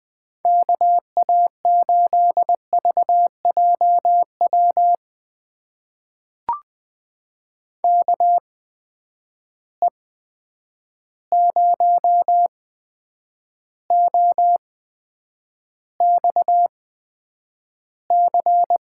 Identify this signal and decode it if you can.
Morse code, heard as a keyed tone, about 20 wpm, spaced as for 9 wpm, 700 Hz. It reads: KA8VJW KE0OXC
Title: USA Call Signs - 6 Characters - 8x Character Spacing - 20wpm